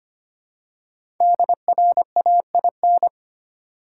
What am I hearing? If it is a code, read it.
Morse code, heard as a keyed tone, about 25 wpm, 700 Hz. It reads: DRAIN
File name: 5 Letter Words - First 3 Letters Given - 25wpm